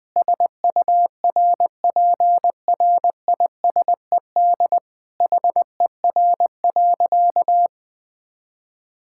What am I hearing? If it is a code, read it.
Morse code, heard as a keyed tone, about 20 wpm, 700 Hz. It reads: SURPRISED HER.
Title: Sentences with 4-Word Phrases - Spoken Beginning - 20wpm